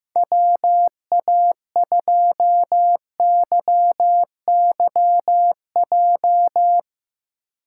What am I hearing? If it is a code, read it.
Morse code, heard as a keyed tone, about 15 wpm, 700 Hz. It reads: WA2YYJ
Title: USA Call Signs - 3x Character Spacing - 15wpm